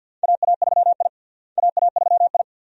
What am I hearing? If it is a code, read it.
Morse code, heard as a keyed tone, about 48 wpm, 700 Hz. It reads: AA3I AA3I
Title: USA Call Signs - 4 Characters - Repeated 2x - 50wpm